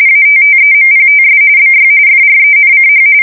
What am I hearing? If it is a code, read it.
RTTY, about 45.45 baud. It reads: ATTEMPTS BY FRENCH
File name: fsk170 / USS-Massachusetts